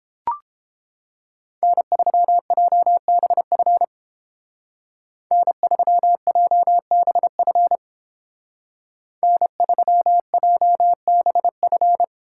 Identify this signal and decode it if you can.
Morse code, heard as a keyed tone, about 29 wpm, 700 Hz. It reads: N3JBF N3JBF N3JBF